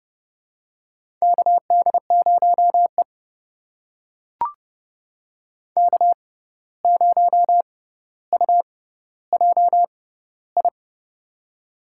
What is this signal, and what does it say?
Morse code, heard as a keyed tone, about 30 wpm, spaced as for 14 wpm, 700 Hz. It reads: KD0E K0UJI
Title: USA Call Signs - 6x Character Spacing - 30wpm